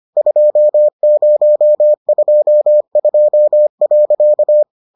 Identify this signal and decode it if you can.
Morse code, heard as a keyed tone, about 25 wpm, 600 Hz. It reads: 2022.